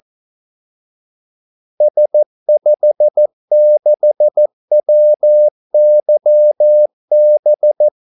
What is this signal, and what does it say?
Morse code, heard as a keyed tone, about 14 wpm, 600 Hz. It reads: S56WYB
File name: Callsigns-14wpm